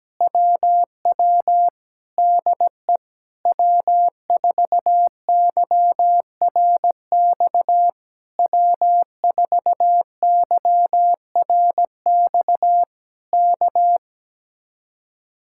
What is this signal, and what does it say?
Morse code, heard as a keyed tone, about 17 wpm, 700 Hz. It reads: WW DE W4YRX W4YRX K
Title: Speed-Racing Common QSO Phrases Encore - 17wpm